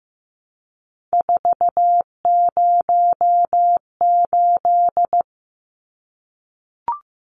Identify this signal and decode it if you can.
Morse code, heard as a keyed tone, about 15 wpm, 700 Hz. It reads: 408
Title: Numbers - 3-Digit 15wpm